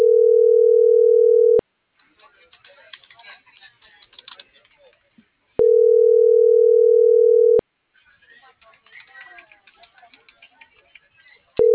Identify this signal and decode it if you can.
Ringback tone (North America), 440 + 480 Hz.